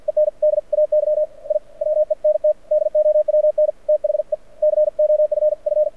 Morse code, heard as a keyed tone, about 53 wpm, spaced as for 47 wpm, 600 Hz. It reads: ENDAY I WENT DOWN THE COFF
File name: archived / CRSnet